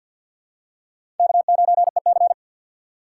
Morse code, heard as a keyed tone, about 53 wpm, 700 Hz. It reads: K9EC
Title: USA Call Signs - 4 Characters - 2x Character Spacing - 50wpm